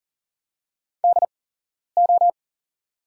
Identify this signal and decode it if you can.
Morse code, heard as a keyed tone, about 40 wpm, spaced as for 11 wpm, 700 Hz. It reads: DO
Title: 8x Character Spacing - Top 200 Words - 40wpm